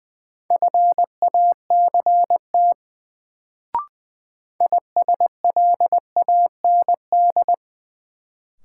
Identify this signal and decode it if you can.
Morse code, heard as a keyed tone, about 21 wpm, 700 Hz. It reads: FACT ISLAND
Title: Sentences from Top 500 Words - New Word Drill - 20wpm